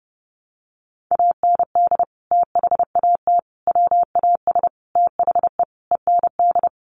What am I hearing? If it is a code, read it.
Morse code, heard as a keyed tone, about 30 wpm, 700 Hz. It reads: AND THAT WAS THE END
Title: Sentences from Top 200 Words 30wpm